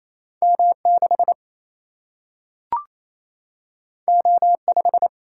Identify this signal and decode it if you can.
Morse code, heard as a keyed tone, about 28 wpm, 700 Hz. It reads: M6 O5
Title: Single-Letter Single-Number 28wpm